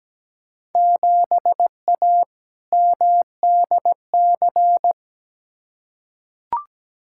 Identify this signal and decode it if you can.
Morse code, heard as a keyed tone, about 17 wpm, 700 Hz. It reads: 7A MDC
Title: Field Day Exchanges Encore - 17wpm